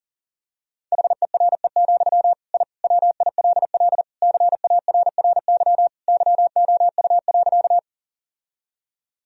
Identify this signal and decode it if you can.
Morse code, heard as a keyed tone, about 40 wpm, 700 Hz. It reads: HERE, I WILL CARRY YOU.
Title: Sentences from Top 1000 Words - Review Part 1 - 40wpm